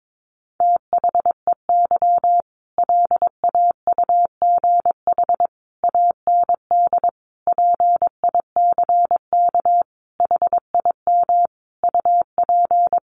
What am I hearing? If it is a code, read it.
Morse code, heard as a keyed tone, about 22 wpm, 700 Hz. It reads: THEY LAUGH AND PICK HIM UP
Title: Sentences from Top 1000 Words 22wpm